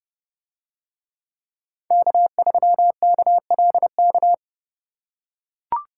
Morse code, heard as a keyed tone, about 28 wpm, 700 Hz. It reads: K3KLK